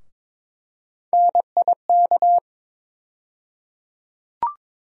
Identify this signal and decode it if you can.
Morse code, heard as a keyed tone, about 22 wpm, 700 Hz. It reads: NIK
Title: USA Call Sign Postfixes - 3 Letters 22wpm